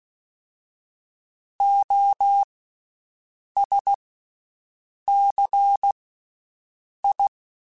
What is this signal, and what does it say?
Morse code, heard as a keyed tone, about 16 wpm, spaced as for 6 wpm, 800 Hz. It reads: OSCI
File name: Special INGs / Special INGs-mp3s / VerySlowToFaster Ing @15